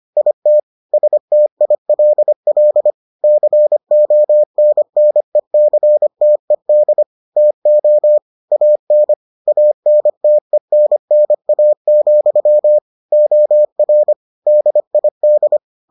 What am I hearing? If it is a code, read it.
Morse code, heard as a keyed tone, about 25 wpm, 600 Hz. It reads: IT STILL CONNECTED TO AN ANTENNA, OR DID